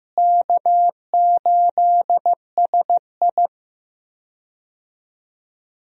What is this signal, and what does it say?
Morse code, heard as a keyed tone, about 15 wpm, 700 Hz. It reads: K8SI
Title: USA Call Signs - 4 Characters - Repeated 3x - 15wpm